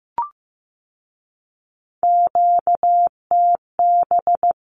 Morse code, heard as a keyed tone, about 15 wpm, 700 Hz. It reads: QTB
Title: Q-Codes with Spoken Meaning - 15wpm